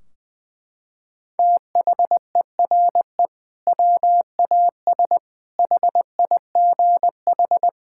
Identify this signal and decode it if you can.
Morse code, heard as a keyed tone, about 20 wpm, 700 Hz. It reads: THERE WAS HIGH
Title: Sentences from Top 1600 Words 20wpm